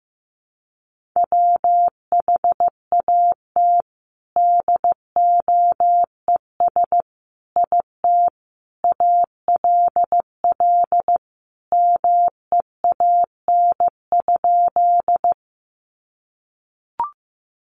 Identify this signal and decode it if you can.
Morse code, heard as a keyed tone, about 15 wpm, 700 Hz. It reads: WHAT DOES IT ALL MEAN?
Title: Sentences from Top 200 Words 15wpm